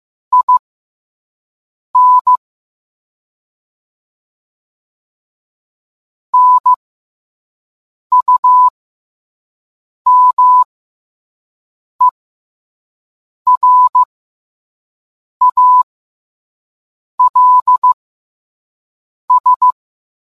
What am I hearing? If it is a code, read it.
Morse code, heard as a keyed tone, about 15 wpm, spaced as for 5 wpm, 1000 Hz. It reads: IN NUMERALS